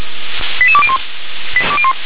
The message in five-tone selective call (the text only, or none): 88201